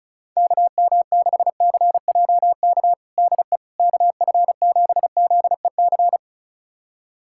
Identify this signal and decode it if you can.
Morse code, heard as a keyed tone, about 36 wpm, 700 Hz. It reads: KM6CJK DE KF7ZEC
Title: USA Callsign DE Callsign - 6 Characters - 35wpm